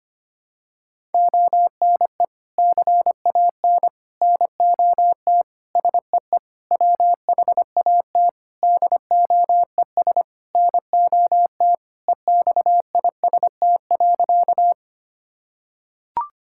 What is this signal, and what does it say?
Morse code, heard as a keyed tone, about 25 wpm, 700 Hz. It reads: ONE CAN NOT SEE WHAT DOES NOT EXIST.
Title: Sentences from Top 1000 Words - Review Part 5 - 25wpm